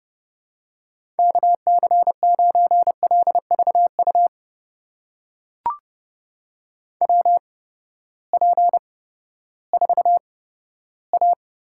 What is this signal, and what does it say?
Morse code, heard as a keyed tone, about 30 wpm, spaced as for 15 wpm, 700 Hz. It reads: KC9LVU WP4A
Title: USA Call Signs - 6 Characters - 8x Character Spacing - 30wpm